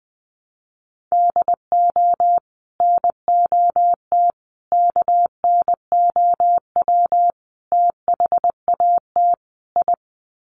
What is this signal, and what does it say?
Morse code, heard as a keyed tone, about 20 wpm, 700 Hz. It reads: DO NOT KNOW THAT I
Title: Sets of 5 Words Encore2 20wpm